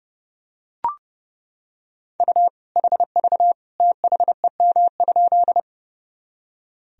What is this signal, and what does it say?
Morse code, heard as a keyed tone, about 31 wpm, 700 Hz. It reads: U HV THEM?
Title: Sentences from Top 100 Words and Abbreviations 30wpm